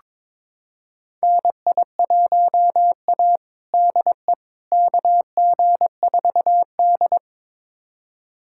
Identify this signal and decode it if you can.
Morse code, heard as a keyed tone, about 22 wpm, 700 Hz. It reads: NI1A DE KG4D
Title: USA Callsign DE Callsign - 4 Characters - 22wpm